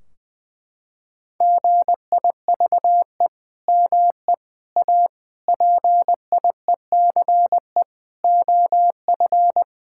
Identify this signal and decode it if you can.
Morse code, heard as a keyed tone, about 20 wpm, 700 Hz. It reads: GIVE ME A PIECE OF
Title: Sentences from Top 1600 Words 20wpm